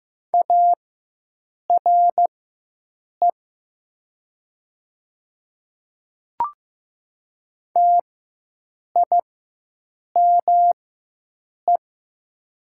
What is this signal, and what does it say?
Morse code, heard as a keyed tone, about 15 wpm, spaced as for 7 wpm, 700 Hz. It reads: ARE TIME
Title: 4x Character Spacing - Top 100 Words - 15wpm